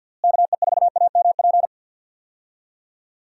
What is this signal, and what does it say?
Morse code, heard as a keyed tone, about 52 wpm, 700 Hz. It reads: KE4AMP
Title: USA Call Signs - 6 Characters - Repeated 2x - 50wpm